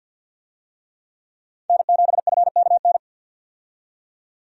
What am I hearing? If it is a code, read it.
Morse code, heard as a keyed tone, about 50 wpm, 700 Hz. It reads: N7FKN
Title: USA Call Signs - 5 Characters - 50wpm